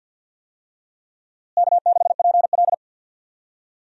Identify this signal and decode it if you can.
Morse code, heard as a keyed tone, about 51 wpm, 700 Hz. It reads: K6PL
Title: USA Call Signs - 4 Characters - Repeated 2x - 50wpm